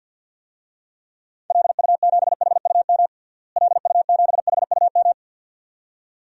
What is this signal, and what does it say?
Morse code, heard as a keyed tone, about 50 wpm, 700 Hz. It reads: LU7HUM LU7HUM
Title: DXCC-100 Call Signs - 6 Characters - Repeated 2x - 50wpm